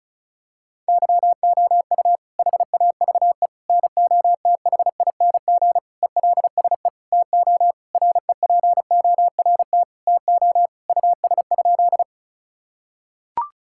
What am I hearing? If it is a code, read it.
Morse code, heard as a keyed tone, about 34 wpm, 700 Hz. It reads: YOU HAVE NOTHING ELSE TO REPORT TO US?
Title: Sentences from Top 1000 Words - Review Part 6 - 35wpm